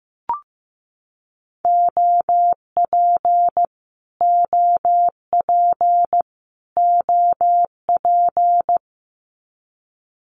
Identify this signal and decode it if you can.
Morse code, heard as a keyed tone, about 15 wpm, 700 Hz. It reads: OP OP OP
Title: Abbreviations Repeated 3x - 15wpm